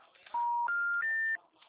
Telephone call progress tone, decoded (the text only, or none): special information tone